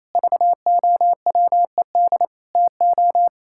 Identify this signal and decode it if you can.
Morse code, heard as a keyed tone, about 28 wpm, 700 Hz. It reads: VOWED TO
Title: Pangram Challenge v2 - 28wpm